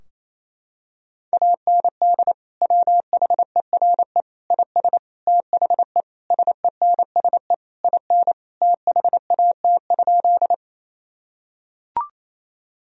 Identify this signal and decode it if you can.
Morse code, heard as a keyed tone, about 28 wpm, 700 Hz. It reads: AND WHERE IS THE SENSE IN THAT?
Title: Sentences from Top 1000 Words - Review Part 4 - 28wpm